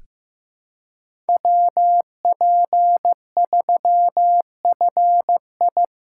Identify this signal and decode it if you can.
Morse code, heard as a keyed tone, about 16 wpm, 700 Hz. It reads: WP3FI
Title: USA Callsign DE Callsign - 5 Characters - 15wpm